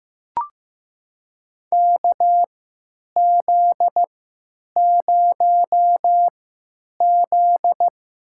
Morse code, heard as a keyed tone, about 15 wpm, spaced as for 9 wpm, 700 Hz. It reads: KZ0Z
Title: USA Call Signs - 4 Characters - 3x Character Spacing - 15wpm